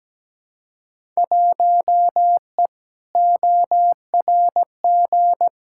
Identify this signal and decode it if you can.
Morse code, heard as a keyed tone, about 17 wpm, 700 Hz. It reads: 1E ORG
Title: Field Day Exchanges - 17wpm